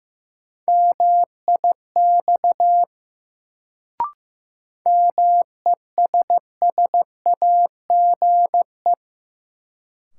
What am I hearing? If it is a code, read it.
Morse code, heard as a keyed tone, about 16 wpm, 700 Hz. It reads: MIX MESSAGE
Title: Sentences from Top 1100 Words - New Word Drill Alt - 15wpm